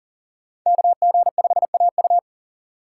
Morse code, heard as a keyed tone, about 40 wpm, 700 Hz. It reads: KG5AU
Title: USA Call Signs - 5 Characters - 2x Character Spacing - 40wpm